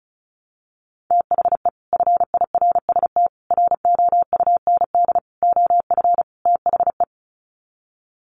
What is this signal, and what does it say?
Morse code, heard as a keyed tone, about 34 wpm, 700 Hz. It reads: THE FIRST ROUND OF THE
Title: Sets of 5 Words Encore 35wpm